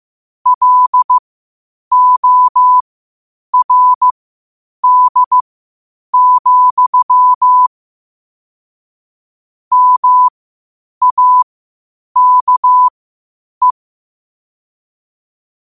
Morse code, heard as a keyed tone, about 15 wpm, spaced as for 8 wpm, 1000 Hz. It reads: LORD, MAKE
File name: qotdfiles / 8wpm